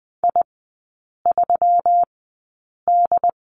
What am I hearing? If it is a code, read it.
Morse code, heard as a keyed tone, about 20 wpm, spaced as for 8 wpm, 700 Hz. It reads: I3D
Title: 138 Lesson 23e - Letter f - Three Characters 20wpm